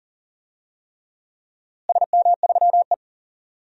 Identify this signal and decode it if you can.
Morse code, heard as a keyed tone, about 40 wpm, 700 Hz. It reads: SM3E
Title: DXCC-100 Call Signs - 4 Characters - Repeated 3x - 40wpm